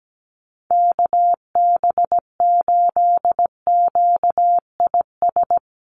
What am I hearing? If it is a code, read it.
Morse code, heard as a keyed tone, about 17 wpm, 700 Hz. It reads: KB8QIS
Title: USA Call Signs Repeated 3x 17wpm